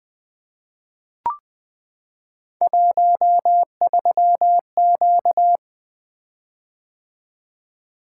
Morse code, heard as a keyed tone, about 20 wpm, 700 Hz. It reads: 13Q
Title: Three Letter-Number Combinations - 20wpm